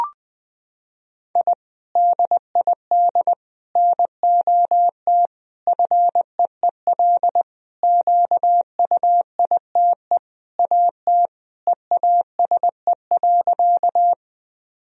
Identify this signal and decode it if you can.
Morse code, heard as a keyed tone, about 20 wpm, 700 Hz. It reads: I DID NOT FEEL QUITE AT EASE.
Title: Sentences from Top 1000 Words - Review Part 1 - 20wpm